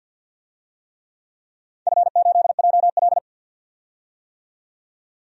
Morse code, heard as a keyed tone, about 53 wpm, 700 Hz. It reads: F8JL
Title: DXCC-100 Call Signs - 4 Characters - 50wpm